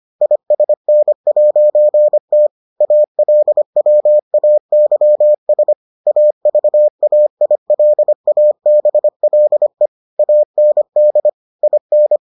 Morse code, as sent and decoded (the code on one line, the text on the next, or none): .. ... -. .----. - / .- .-.. .-- .- -.-- ... / .- ...- .- .. .-.. .- -... .-.. / .- -. -.. / .. -.
ISN'T ALWAYS AVAILABLE AND IN